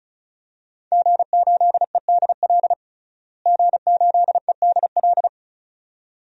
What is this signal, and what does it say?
Morse code, heard as a keyed tone, about 36 wpm, 700 Hz. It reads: G8EDL G8EDL
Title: DXCC-100 Call Signs - 5 Characters - Repeated 2x - 35wpm